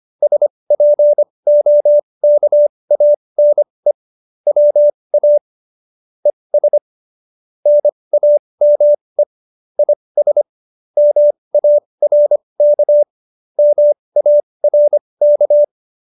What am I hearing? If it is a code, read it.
Morse code, heard as a keyed tone, about 25 wpm, spaced as for 20 wpm, 590 Hz. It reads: SPOKANE WA ES NAME IS MARK MARK